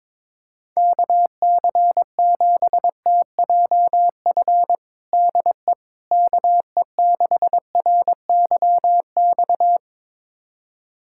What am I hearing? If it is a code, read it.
Morse code, heard as a keyed tone, about 23 wpm, 700 Hz. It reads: KC7TJF DE KE6RYX
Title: USA Callsign DE Callsign - 6 Characters - 22wpm